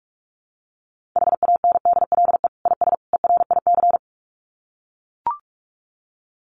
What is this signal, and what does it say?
Morse code, heard as a keyed tone, about 44 wpm, 700 Hz. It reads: HANDLE IS ERIC
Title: Common QSO Phrases 45wpm